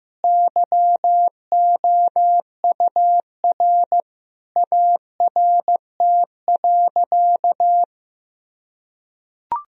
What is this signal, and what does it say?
Morse code, heard as a keyed tone, about 15 wpm, 700 Hz. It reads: YOUR ART.
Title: Sentences from Top 1000 Words - Review Part 3 - 15wpm